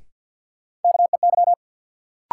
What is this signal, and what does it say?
Morse code, heard as a keyed tone, about 52 wpm, 700 Hz. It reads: KEY
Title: Sentences from Top 600 Words - New Word Drill - 50wpm